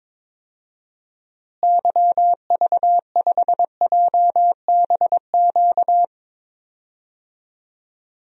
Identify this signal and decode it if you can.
Morse code, heard as a keyed tone, about 22 wpm, 700 Hz. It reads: YV5JBQ
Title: DXCC-100 Call Signs - 6 Characters - Repeated 2x - 22wpm